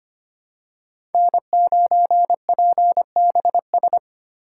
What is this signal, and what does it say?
Morse code, heard as a keyed tone, about 25 wpm, 700 Hz. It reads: N9PBS
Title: USA Call Signs - 5 Characters - 4x Character Spacing - 25wpm